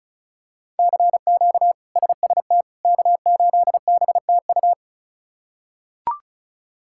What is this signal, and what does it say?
Morse code, heard as a keyed tone, about 36 wpm, 700 Hz. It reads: CQ SST K8BTU
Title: K1USN SST QSO Practice - 35wpm